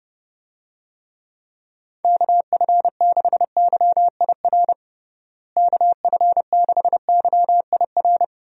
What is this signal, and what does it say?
Morse code, heard as a keyed tone, about 32 wpm, 700 Hz. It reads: KF6YIR KF6YIR
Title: USA Call Signs - 6 Characters - Repeated 2x - 30wpm